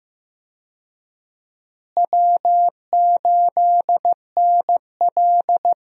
Morse code, heard as a keyed tone, about 15 wpm, 700 Hz. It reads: W8NL
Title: USA Call Signs - 4 Characters - Repeated 3x - 15wpm